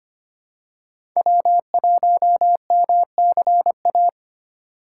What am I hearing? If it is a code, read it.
Morse code, heard as a keyed tone, about 25 wpm, 700 Hz. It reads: W1MCA
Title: USA Call Signs - 5 Characters - 4x Character Spacing - 25wpm